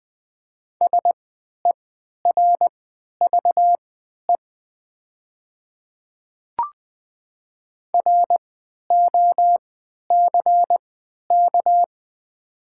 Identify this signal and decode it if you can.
Morse code, heard as a keyed tone, about 20 wpm, spaced as for 11 wpm, 700 Hz. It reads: SERVE ROCK
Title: 3x Character Spacing - Top 500 Words - 20wpm